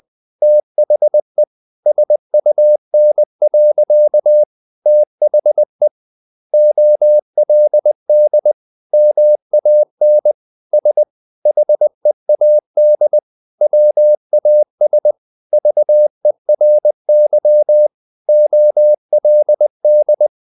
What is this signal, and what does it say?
Morse code, heard as a keyed tone, about 20 wpm, 600 Hz. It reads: THE SUN. THE OLD MAN S HEAD WAS VERY OLD
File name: the old man and the sea / the old man and the sea part 1 20wpm